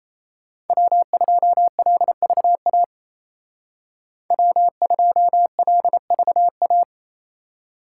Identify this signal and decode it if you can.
Morse code, heard as a keyed tone, about 34 wpm, 700 Hz. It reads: W2LVA W2LVA
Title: Speed Racing - USA 1x3 Call Signs 28wpm